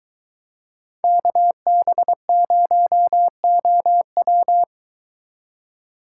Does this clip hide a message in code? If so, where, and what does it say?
Yes: Morse code from 1.0 to 4.6 s, heard as a keyed tone, about 22 wpm, 700 Hz. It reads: KB0OW